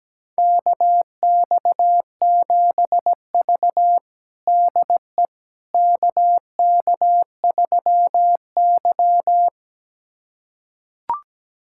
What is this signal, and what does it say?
Morse code, heard as a keyed tone, about 17 wpm, 700 Hz. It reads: KX7V DE KK3Y
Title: USA Callsign DE Callsign - 4 Characters - 17wpm